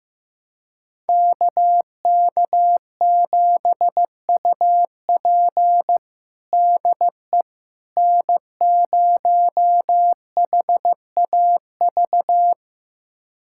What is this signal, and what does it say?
Morse code, heard as a keyed tone, about 15 wpm, 700 Hz. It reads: KK7UP DE N0HAV
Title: USA Callsign DE Callsign - 5 Characters - 15wpm